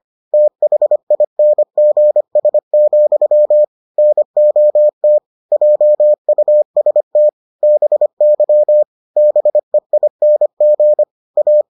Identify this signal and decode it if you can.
Morse code, heard as a keyed tone, about 25 wpm, 600 Hz. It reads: THINGS, NOT JUST BY BEING A